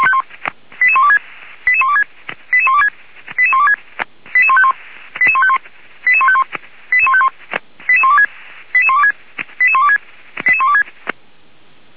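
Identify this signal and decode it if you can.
Five-tone selective call (ZVEI), heard as quick runs of five tones, tones of about 80 ms. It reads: 88126 88126 88126 88126 88151 88151 88151 88151 88126 88126 88126 88126
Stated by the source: ZVEI five-tone call 88151 received by SDR.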